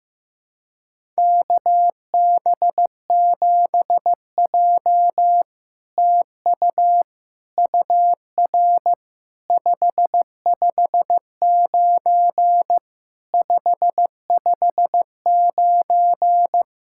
Morse code, heard as a keyed tone, about 15 wpm, 690 Hz. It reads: KB7J TU UR 559 559
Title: SOTA QSO Practice 15wpm